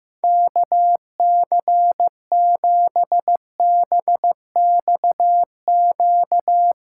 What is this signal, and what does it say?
Morse code, heard as a keyed tone, about 15 wpm, 700 Hz. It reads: KC7BXQ
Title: USA Callsign DE Callsign 15wpm